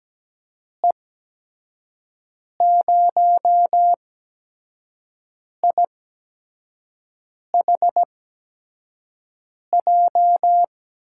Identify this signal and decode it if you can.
Morse code, heard as a keyed tone, about 17 wpm, spaced as for 5 wpm, 700 Hz. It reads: E0IHJ
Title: USA Call Signs - 6 Characters - 8x Character Spacing - 17wpm